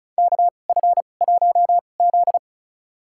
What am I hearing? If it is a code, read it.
Morse code, heard as a keyed tone, about 34 wpm, spaced as for 25 wpm, 700 Hz. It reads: KF1Z
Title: USA Call Signs - 2x Character Spacing - 35wpm